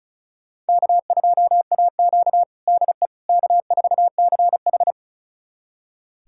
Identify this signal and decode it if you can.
Morse code, heard as a keyed tone, about 34 wpm, 700 Hz. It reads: K2AQ DE K4CH
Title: USA Callsign DE Callsign - 4 Characters - 35wpm